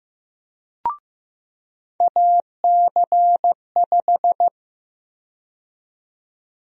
Morse code, heard as a keyed tone, about 15 wpm, 700 Hz. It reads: AC5
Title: USA Call Sign Prefixes - 15wpm